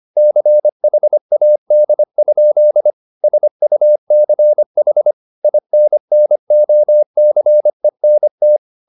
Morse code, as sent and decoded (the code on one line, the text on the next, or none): -.-. .... .- -.. ..--.. / ... ..- -.-. .... / .. -. -. --- -.-. -. -
CHAD? SUCH INNOCENT